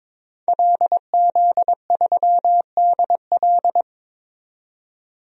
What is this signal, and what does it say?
Morse code, heard as a keyed tone, about 22 wpm, 700 Hz. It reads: LZ3DL
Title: DXCC-100 Call Signs - 5 Characters - 22wpm